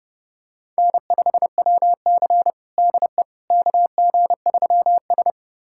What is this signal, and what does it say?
Morse code, heard as a keyed tone, about 32 wpm, 700 Hz. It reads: N5WC DE KG3S
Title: USA Callsign DE Callsign - 4 Characters - 30wpm